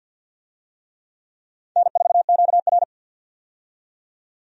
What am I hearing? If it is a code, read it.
Morse code, heard as a keyed tone, about 49 wpm, 700 Hz. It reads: N4QR
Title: USA Call Signs - 4 Characters - 50wpm